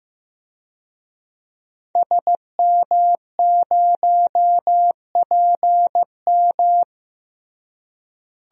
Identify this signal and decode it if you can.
Morse code, heard as a keyed tone, about 15 wpm, 700 Hz. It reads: SM0PM
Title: DXCC-100 Call Signs - 5 Characters - Repeated 2x - 15wpm